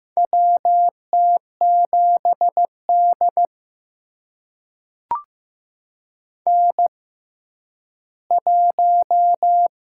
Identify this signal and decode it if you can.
Morse code, heard as a keyed tone, about 15 wpm, spaced as for 10 wpm, 700 Hz. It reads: WT7D N1
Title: USA Call Signs - 4 Characters - 6x Character Spacing - 15wpm